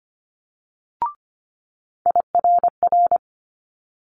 Morse code, heard as a keyed tone, about 25 wpm, 700 Hz. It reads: IRR
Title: Multi-letter Phonemes Quick Fire 25wpm